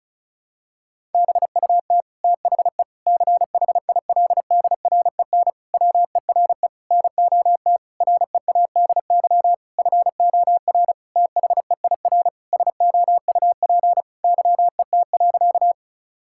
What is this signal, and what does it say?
Morse code, heard as a keyed tone, about 36 wpm, 700 Hz. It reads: BUT THE CHILDREN WERE NOT READY FOR THEIR SOUP YET.
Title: Sentences from Top 1000 Words - Review Part 6 - 35wpm